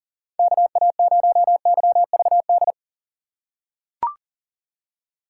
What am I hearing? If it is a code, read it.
Morse code, heard as a keyed tone, about 40 wpm, 700 Hz. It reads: KA0YVD
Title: USA Call Signs - 6 Characters - 4x Character Spacing - 40wpm